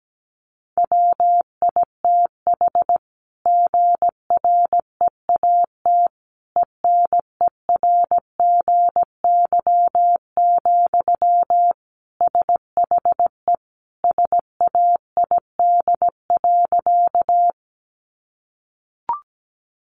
Morse code, heard as a keyed tone, about 17 wpm, 700 Hz. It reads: WITH GREAT ENERGY, SHE SAID.
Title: Sentences from Top 900 Words 17wpm